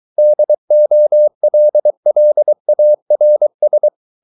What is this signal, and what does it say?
Morse code, heard as a keyed tone, about 23 wpm, 600 Hz. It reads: DOLLARS